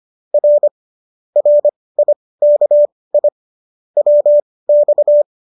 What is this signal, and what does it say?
Morse code, heard as a keyed tone, about 25 wpm, spaced as for 18 wpm, 590 Hz. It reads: R RIKI WX